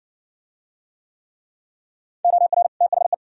Morse code, heard as a keyed tone, about 60 wpm, 700 Hz. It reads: OR THE